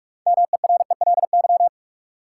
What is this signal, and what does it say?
Morse code, heard as a keyed tone, about 46 wpm, 700 Hz. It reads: MERELY
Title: Sentences from Top 1400 Words - New Word Drill Alt - 45wpm